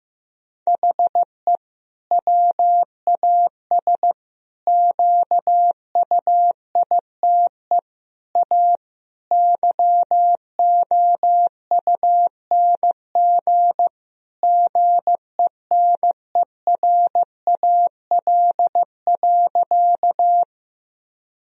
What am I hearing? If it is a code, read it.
Morse code, heard as a keyed tone, about 15 wpm, 700 Hz. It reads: HE WAS QUITE A YOUNG GENERAL.
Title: Sentences from Top 1000 Words - Review Part 3 - 15wpm